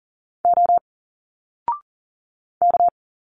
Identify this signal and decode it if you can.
Morse code, heard as a keyed tone, about 40 wpm, 700 Hz. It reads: O K
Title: Single Letters 40wpm